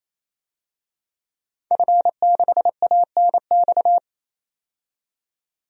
Morse code, heard as a keyed tone, about 28 wpm, 700 Hz. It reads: F6ANX
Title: DXCC-100 Call Signs - 5 Characters - 28wpm